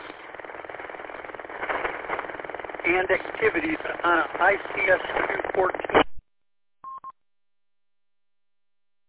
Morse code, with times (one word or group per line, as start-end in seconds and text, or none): none